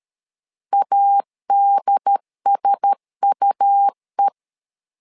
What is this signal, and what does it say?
Morse code, heard as a keyed tone, about 13 wpm, 800 Hz. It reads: ADSUE
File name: LEZIONI / 5 LBVC / 5 L 12wpm 1913